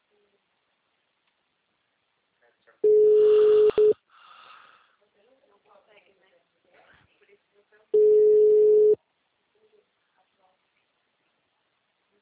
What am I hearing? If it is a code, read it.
Ringback tone (Europe, 425 Hz), with speech.